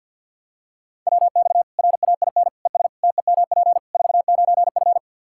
Morse code, heard as a keyed tone, about 45 wpm, 700 Hz. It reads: WX RAIN ES TEMP 49F
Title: Practice QSO - Exchange 2 - Info - 50wpm